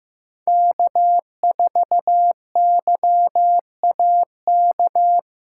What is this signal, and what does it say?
Morse code, heard as a keyed tone, about 18 wpm, 700 Hz. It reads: K4YAK
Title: Speed Racing - USA 1x3 Call Signs 15wpm